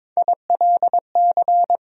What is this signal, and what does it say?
Morse code, heard as a keyed tone, about 22 wpm, 700 Hz. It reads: ILC
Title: USA Call Sign Postfixes - 3 Letters 22wpm